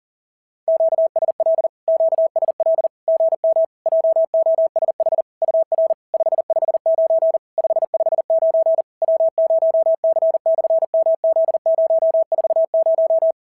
Morse code, heard as a keyed tone, about 40 wpm, 650 Hz. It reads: QSL QSL GM JOSH UR 559 559 W0C/MZ040
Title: SOTA QSO Practice 40wpm